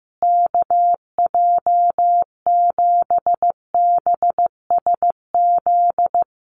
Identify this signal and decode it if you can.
Morse code, heard as a keyed tone, about 15 wpm, 700 Hz. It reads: KJ7BSZ
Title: USA Call Signs Repeated 2x 15wpm